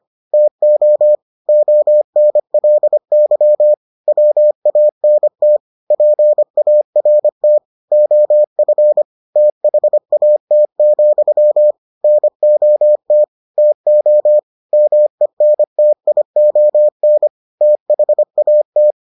Morse code, heard as a keyed tone, about 25 wpm, 600 Hz. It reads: TO ONLY WANT PART OF THAT, NOT TO MENTION THAT